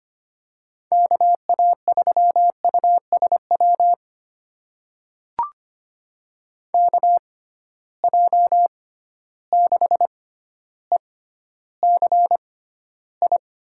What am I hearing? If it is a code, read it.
Morse code, heard as a keyed tone, about 25 wpm, spaced as for 13 wpm, 700 Hz. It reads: KA3USW KJ6ECI
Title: USA Call Signs - 6 Characters - 6x Character Spacing - 25wpm